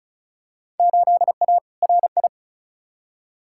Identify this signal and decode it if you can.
Morse code, heard as a keyed tone, about 36 wpm, 700 Hz. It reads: 8A RI
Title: Field Day Exchanges Encore - 35wpm